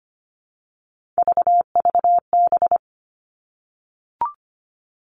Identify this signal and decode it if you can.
Morse code, heard as a keyed tone, about 25 wpm, 700 Hz. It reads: VVB
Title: Three Challenging Letter-Number Combinations 25wpm